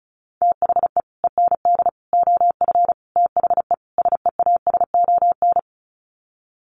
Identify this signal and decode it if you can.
Morse code, heard as a keyed tone, about 34 wpm, 700 Hz. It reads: THE END OF THE SEASON
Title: Sets of 5 Words Encore 35wpm